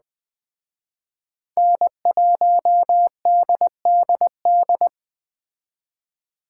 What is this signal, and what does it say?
Morse code, heard as a keyed tone, about 21 wpm, 700 Hz. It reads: N1DDD
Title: Speed Racing - USA 1x3 Call Signs 17wpm